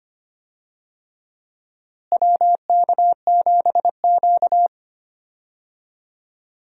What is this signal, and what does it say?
Morse code, heard as a keyed tone, about 25 wpm, 700 Hz. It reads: WK7Q
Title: USA Call Signs - 4 Characters - Repeated 3x - 25wpm